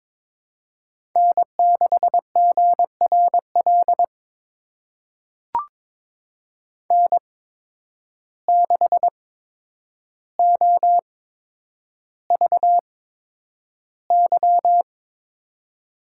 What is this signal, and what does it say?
Morse code, heard as a keyed tone, about 23 wpm, spaced as for 10 wpm, 700 Hz. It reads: N6GRL N6OVY
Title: USA Call Signs - 8x Character Spacing - 22wpm